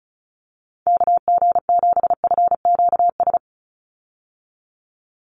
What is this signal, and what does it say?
Morse code, heard as a keyed tone, about 34 wpm, 700 Hz. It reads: KG7FQS
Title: USA Call Signs 35wpm